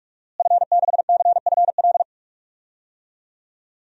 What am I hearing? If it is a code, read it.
Morse code, heard as a keyed tone, about 46 wpm, 700 Hz. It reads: F6CFL
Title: DXCC-100 Call Signs - 5 Characters - Repeated 2x - 45wpm